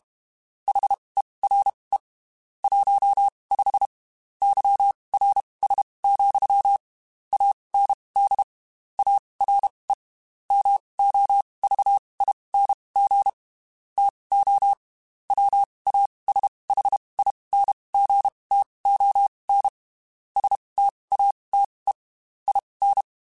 Morse code, heard as a keyed tone, about 32 wpm, spaced as for 23 wpm, 800 Hz. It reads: HERE 15 YRS, AND ARE MOVING TO WASHINGTON STATE IN
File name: QSOs / QSOs-mp3s / WD0BGP 1@30